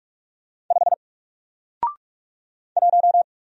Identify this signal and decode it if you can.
Morse code, heard as a keyed tone, about 44 wpm, 700 Hz. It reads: 5 1
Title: Single Character (Letter:Number:Punctuation:Prosign) 45wpm